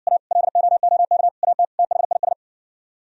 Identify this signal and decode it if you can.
Morse code, heard as a keyed tone, about 67 wpm, 700 Hz. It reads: A LOOK AT THIS